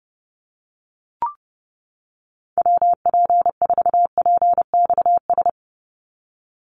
Morse code, heard as a keyed tone, about 30 wpm, 700 Hz. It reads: WP4PXS
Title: USA Call Signs 30wpm